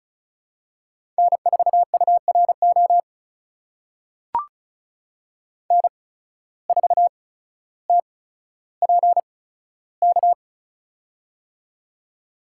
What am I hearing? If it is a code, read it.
Morse code, heard as a keyed tone, about 34 wpm, spaced as for 15 wpm, 700 Hz. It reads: N4URO N4TPK
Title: USA Call Signs - 5 Characters - 8x Character Spacing - 35wpm